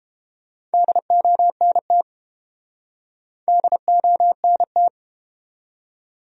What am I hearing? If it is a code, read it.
Morse code, heard as a keyed tone, about 31 wpm, 700 Hz. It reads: DONT DONT